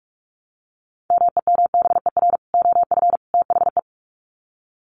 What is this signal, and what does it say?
Morse code, heard as a keyed tone, about 44 wpm, 700 Hz. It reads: MEMBER OF THE
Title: Sets of 3 Words 45wpm